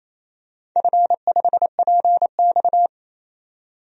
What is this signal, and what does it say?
Morse code, heard as a keyed tone, about 28 wpm, 700 Hz. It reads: F5PX
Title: DXCC-100 Call Signs - 4 Characters - 28wpm